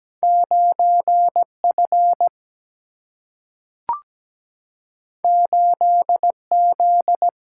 Morse code, heard as a keyed tone, about 17 wpm, 700 Hz. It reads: 9F 8Z
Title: Single-Letter Single-Number 17wpm